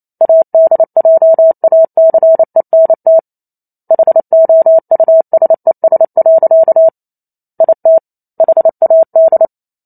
Morse code, heard as a keyed tone, about 28 wpm, 650 Hz. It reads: ADJACENT HOUSES. IT HAD